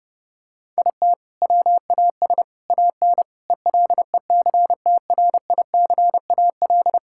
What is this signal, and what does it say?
Morse code, heard as a keyed tone, about 30 wpm, 700 Hz. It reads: IT WAS AN ELECTRICAL
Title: Sentences from Top 1000 Words - Review Part 6 - 30wpm